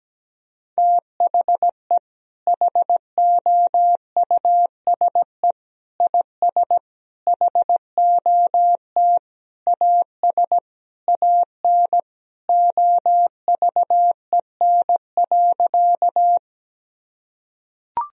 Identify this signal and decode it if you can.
Morse code, heard as a keyed tone, about 17 wpm, 700 Hz. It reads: THE HOUSE IS HOT AS AN OVEN.
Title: Sentences from Top 1000 Words - Review Part 4 - 17wpm